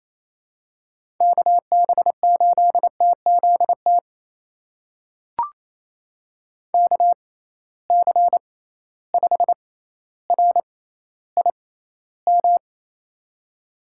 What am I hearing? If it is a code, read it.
Morse code, heard as a keyed tone, about 27 wpm, spaced as for 14 wpm, 700 Hz. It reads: KB8TZT KC5RIM
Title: USA Call Signs - 6 Characters - 6x Character Spacing - 28wpm